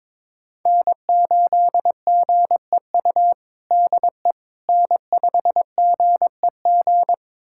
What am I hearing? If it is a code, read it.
Morse code, heard as a keyed tone, about 22 wpm, 700 Hz. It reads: N8GEU DE N5GEG
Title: USA Callsign DE Callsign - 5 Characters - 22wpm